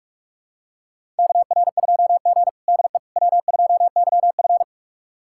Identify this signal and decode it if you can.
Morse code, heard as a keyed tone, about 42 wpm, 700 Hz. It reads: KR2G DE W2YF
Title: USA Callsign DE Callsign - 4 Characters - 45wpm